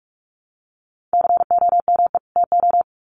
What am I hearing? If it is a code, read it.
Morse code, heard as a keyed tone, about 44 wpm, 700 Hz. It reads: COME TO
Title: Sets of 2 Words 45wpm